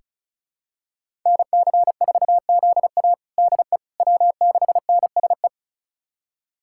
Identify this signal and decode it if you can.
Morse code, heard as a keyed tone, about 36 wpm, 700 Hz. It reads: NC4ZA DE W6NSE
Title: USA Callsign DE Callsign - 5 Characters - 35wpm